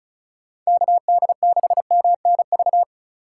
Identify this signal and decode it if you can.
Morse code, heard as a keyed tone, about 36 wpm, 700 Hz. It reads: KD6MNV
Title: USA Call Signs - 6 Characters - 4x Character Spacing - 35wpm